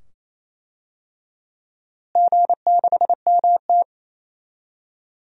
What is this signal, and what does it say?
Morse code, heard as a keyed tone, about 28 wpm, 700 Hz. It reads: G6MT
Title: DXCC-100 Call Signs - 4 Characters - Repeated 2x - 28wpm